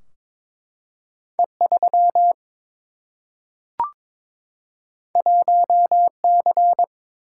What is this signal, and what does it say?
Morse code, heard as a keyed tone, about 22 wpm, 700 Hz. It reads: E3 1C
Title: Single-Letter Single-Number 22wpm